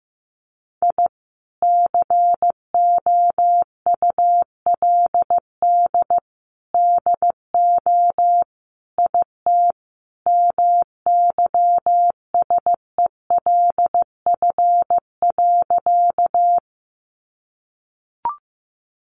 Morse code, heard as a keyed tone, about 15 wpm, 700 Hz. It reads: I COULD DO IT MYSELF.
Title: Sentences from Top 1300 Words 15wpm